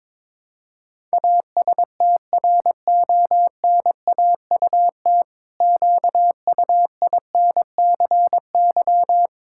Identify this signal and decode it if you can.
Morse code, heard as a keyed tone, about 22 wpm, 700 Hz. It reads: ASTRONAUT QUINCY
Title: Pangram Challenge v2 - 22wpm